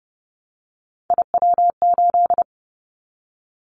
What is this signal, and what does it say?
Morse code, heard as a keyed tone, about 30 wpm, 700 Hz. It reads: IW8
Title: Callsign Prefixes for Top 25 DXCC - 30wpm